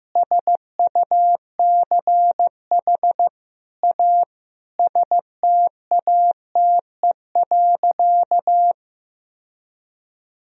Morse code, heard as a keyed tone, about 16 wpm, 700 Hz. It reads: SUCH A STATE.
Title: Sentences from Top 1000 Words - Review Part 2 - 15wpm